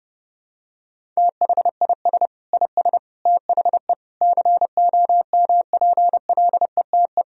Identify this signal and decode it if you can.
Morse code, heard as a keyed tone, about 30 wpm, 700 Hz. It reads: THIS IS THE COMPLETE
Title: Sentences from Top 1000 Words - Review Part 2 - 30wpm